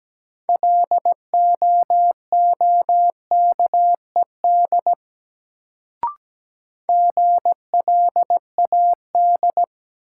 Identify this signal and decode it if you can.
Morse code, heard as a keyed tone, about 17 wpm, 700 Hz. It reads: LOOKED GLAD
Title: Sentences from Top 800 Words - New Word Drill - 17wpm